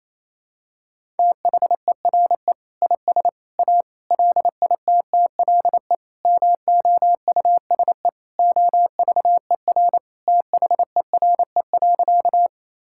Morse code, heard as a keyed tone, about 28 wpm, 700 Hz. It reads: THERE IS A LITTLE MOUSE OVER THERE.
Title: Sentences from Top 1000 Words - Review Part 5 - 28wpm